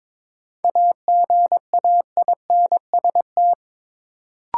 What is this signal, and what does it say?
Morse code, heard as a keyed tone, about 21 wpm, 700 Hz. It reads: AGAINST
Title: Sets of 2 Words - New Word Drill - 22wpm